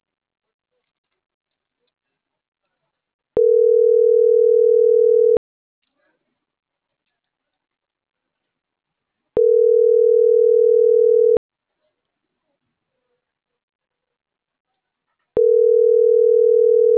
Ringback tone (North America), 440 + 480 Hz.